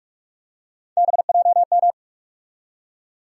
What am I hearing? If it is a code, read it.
Morse code, heard as a keyed tone, about 46 wpm, 700 Hz. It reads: BJM
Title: USA Call Sign Postfixes - 3 Letters 45wpm